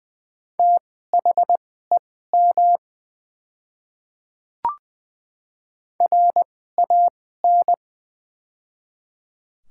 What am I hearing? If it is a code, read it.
Morse code, heard as a keyed tone, about 20 wpm, spaced as for 14 wpm, 700 Hz. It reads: THEM RAN
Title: 2x Character Spacing - Top 500 Words - 20wpm